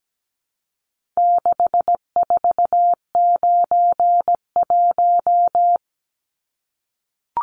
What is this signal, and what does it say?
Morse code, heard as a keyed tone, about 17 wpm, 700 Hz. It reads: 6491
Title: Numbers - 4-Digit 17wpm